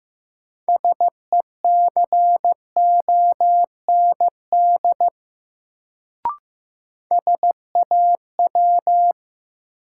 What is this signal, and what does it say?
Morse code, heard as a keyed tone, about 15 wpm, 700 Hz. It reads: SECOND SAW
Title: Sentences from Top 300 Words - New Word Drill - 15wpm